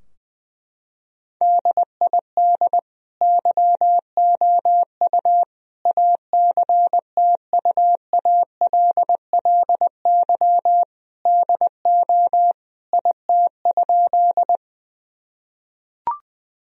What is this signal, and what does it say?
Morse code, heard as a keyed tone, about 21 wpm, 700 Hz. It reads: DID YOU ACTUALLY DO IT?
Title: Sentences from Top 1000 Words - Review Part 4 - 20wpm